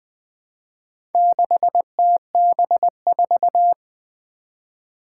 Morse code, heard as a keyed tone, about 20 wpm, 700 Hz. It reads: 6TB4
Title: Four Challenging Letter-Number Combinations 20wpm